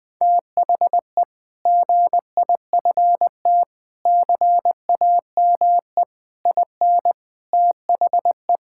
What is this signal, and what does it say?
Morse code, heard as a keyed tone, about 21 wpm, 700 Hz. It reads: THE GIFT CAME IN THE
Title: Sentences with 5-Word Phrases - Spoken Beginning - 20wpm